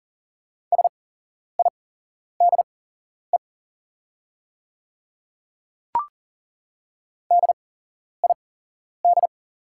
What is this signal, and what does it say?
Morse code, heard as a keyed tone, about 40 wpm, spaced as for 11 wpm, 690 Hz. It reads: SIDE DID
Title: 8x Character Spacing - Top 300 Words - 40wpm